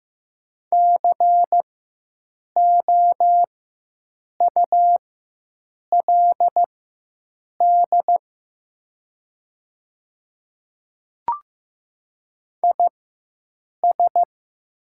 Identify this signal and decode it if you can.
Morse code, heard as a keyed tone, about 15 wpm, spaced as for 7 wpm, 700 Hz. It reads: COULD IS
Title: 4x Character Spacing - Top 100 Words - 15wpm